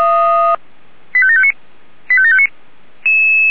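Five-tone selective call (ZVEI), heard as quick runs of five tones, tones of about 70 ms. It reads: 85759 85759
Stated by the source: ZVEI five-tone selective call received with an SDR.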